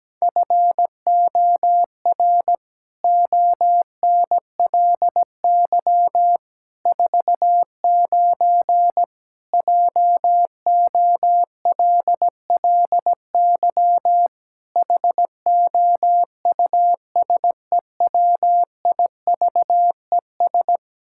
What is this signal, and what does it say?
Morse code, heard as a keyed tone, about 17 wpm, 700 Hz. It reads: FOR ONLY 49 JOLLY HOUSEWIVES
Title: Pangram Challenge v2 - 17wpm